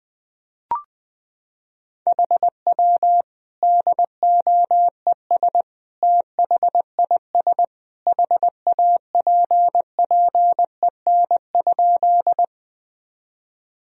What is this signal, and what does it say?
Morse code, heard as a keyed tone, about 20 wpm, 700 Hz. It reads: HW DOES THIS HAPPEN?
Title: Sentences from Top 400 Words and Abbreviations 20wpm